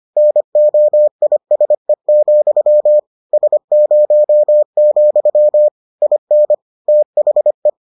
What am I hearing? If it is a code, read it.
Morse code, heard as a keyed tone, about 25 wpm, 600 Hz. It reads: NOISE, S0, IN THE